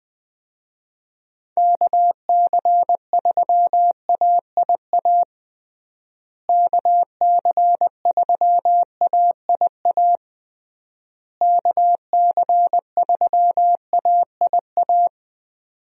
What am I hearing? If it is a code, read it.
Morse code, heard as a keyed tone, about 20 wpm, 700 Hz. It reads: KC3AIA KC3AIA KC3AIA
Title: USA Call Signs - 6 Characters - Repeated 3x - 20wpm